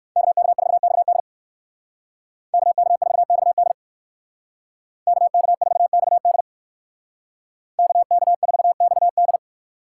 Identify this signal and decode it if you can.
Morse code, heard as a keyed tone, about 61 wpm, 700 Hz. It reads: KK4XD KK4XD KK4XD KK4XD